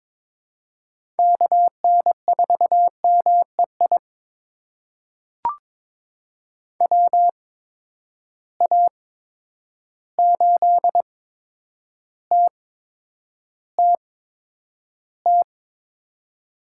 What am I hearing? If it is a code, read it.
Morse code, heard as a keyed tone, about 21 wpm, spaced as for 9 wpm, 700 Hz. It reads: KN4MEI WA8TTT
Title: USA Call Signs - 6 Characters - 8x Character Spacing - 22wpm